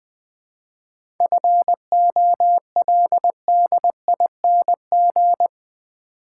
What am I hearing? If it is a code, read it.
Morse code, heard as a keyed tone, about 20 wpm, 690 Hz. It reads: FOLDING
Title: 7 Letter Words - First 3 Letters Given - 20wpm